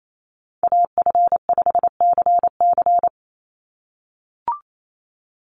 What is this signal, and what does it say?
Morse code, heard as a keyed tone, about 28 wpm, 700 Hz. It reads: AF5CC
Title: Call Signs - 28wpm